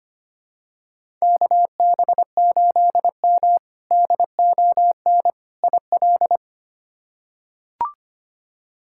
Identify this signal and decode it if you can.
Morse code, heard as a keyed tone, about 25 wpm, 700 Hz. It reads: KB8M DON IL
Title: K1USN SST QSO Practice - 25wpm